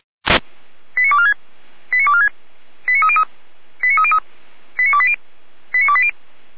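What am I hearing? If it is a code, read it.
Five-tone selective call (ZVEI), heard as quick runs of five tones, tones of about 70 ms. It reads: 89246 89246 89303 89303 89280 89280
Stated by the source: ZVEI five-tone selective call 89230 received by SDR, archived on the Feuerwehr Orlamünde fire brigade server.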